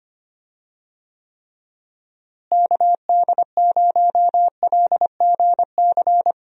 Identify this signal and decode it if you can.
Morse code, heard as a keyed tone, about 25 wpm, 700 Hz. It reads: KD0LGC
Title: USA Call Signs - 6 Characters - Repeated 2x - 25wpm